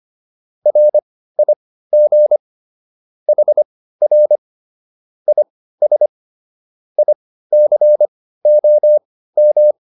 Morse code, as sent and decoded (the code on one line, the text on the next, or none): .-. .. --. / .... .-. / .. ... / .. -.-. --- --
RIG HR IS ICOM